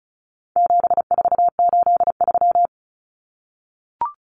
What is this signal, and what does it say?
Morse code, heard as a keyed tone, about 34 wpm, 700 Hz. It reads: 7483
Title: Common Cut Numbers - 4-Digit 35wpm